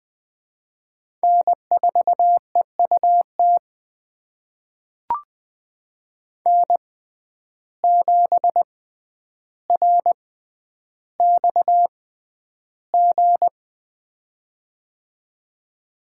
Morse code, heard as a keyed tone, about 20 wpm, spaced as for 10 wpm, 700 Hz. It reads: N4EUT N7RXG
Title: USA Call Signs - 5 Characters - 6x Character Spacing - 20wpm